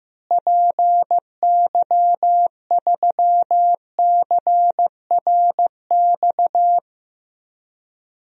Morse code, heard as a keyed tone, about 15 wpm, 700 Hz. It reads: PY3CRX
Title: DXCC-100 Call Signs - 6 Characters - Repeated 3x - 15wpm